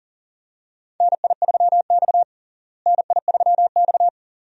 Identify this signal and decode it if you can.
Morse code, heard as a keyed tone, about 40 wpm, 700 Hz. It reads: NI3X NI3X
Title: USA Call Signs - 4 Characters - Repeated 2x - 40wpm